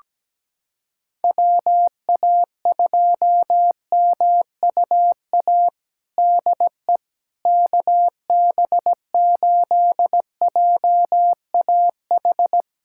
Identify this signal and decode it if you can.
Morse code, heard as a keyed tone, about 17 wpm, 700 Hz. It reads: WA2MUA DE KB8JAH
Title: USA Callsign DE Callsign 17wpm